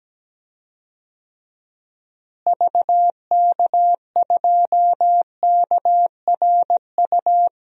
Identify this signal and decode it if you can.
Morse code, heard as a keyed tone, about 17 wpm, 700 Hz. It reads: VK2KRU
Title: DXCC-100 Call Signs - 6 Characters - Repeated 2x - 17wpm